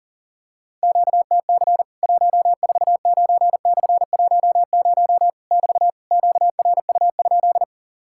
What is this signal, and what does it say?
Morse code, heard as a keyed tone, about 40 wpm, 700 Hz. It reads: QTC 149/10 <BT> QRU?
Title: Worked All Europe DX Contest - QTC Exchange - 40wpm